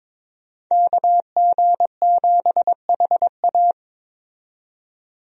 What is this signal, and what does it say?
Morse code, heard as a keyed tone, about 21 wpm, 700 Hz. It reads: KG7HA
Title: USA Call Signs - 5 Characters - 22wpm